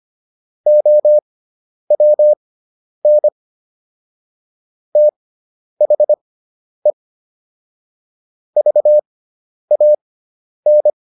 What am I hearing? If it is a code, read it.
Morse code, heard as a keyed tone, about 25 wpm, spaced as for 10 wpm, 600 Hz. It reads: OWN THE VAN